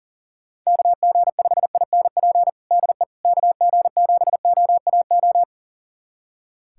Morse code, heard as a keyed tone, about 40 wpm, 700 Hz. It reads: KG5INP DE KG7OAO
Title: USA Callsign DE Callsign - 6 Characters - 40wpm